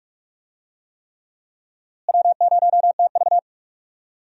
Morse code, heard as a keyed tone, about 43 wpm, 700 Hz. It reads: W0TV
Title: USA Call Signs - 4 Characters - Repeated 3x - 45wpm